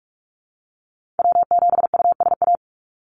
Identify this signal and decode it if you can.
Morse code, heard as a keyed tone, about 44 wpm, 700 Hz. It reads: W7USA
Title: NA SOTA Activator Callsigns 45wpm